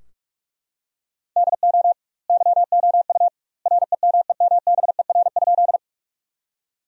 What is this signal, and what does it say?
Morse code, heard as a keyed tone, about 43 wpm, 700 Hz. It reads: DO YOU REMEMBER?
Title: 2x Word-Spacing - Sentences from Top 400 Words - 45wpm